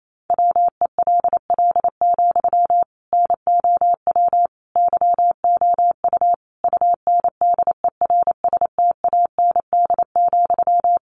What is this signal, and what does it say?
Morse code, heard as a keyed tone, about 28 wpm, 700 Hz. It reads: WELL, NOW YOU UNDERSTAND,
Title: Sentences from Top 600 Words 28wpm